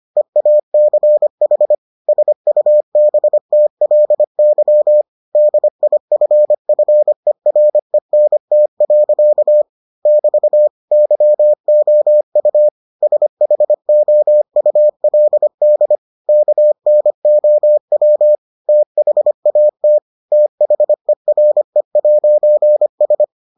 Morse code, heard as a keyed tone, about 25 wpm, 600 Hz. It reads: EACH SUBTLY DIFFERENT. <BT> YOU SHOULD KNOW THAT THERE'S